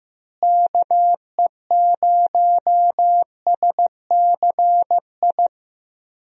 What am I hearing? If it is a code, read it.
Morse code, heard as a keyed tone, about 16 wpm, 700 Hz. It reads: KE0SCI
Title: USA Call Signs - 6 Characters - Repeated 2x - 15wpm